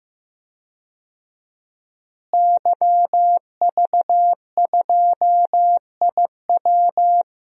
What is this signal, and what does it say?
Morse code, heard as a keyed tone, about 15 wpm, 700 Hz. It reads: YV2IW
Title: DXCC-100 Call Signs - 5 Characters - Repeated 2x - 15wpm